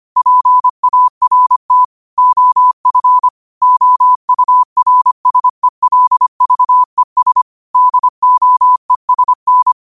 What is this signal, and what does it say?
Morse code, heard as a keyed tone, about 25 wpm, 1000 Hz. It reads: PART OF OURSELVES DOESN